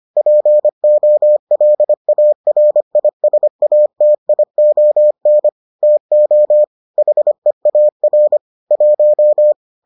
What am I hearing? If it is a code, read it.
Morse code, heard as a keyed tone, about 25 wpm, 600 Hz. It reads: POLARISATION TO HEAR 1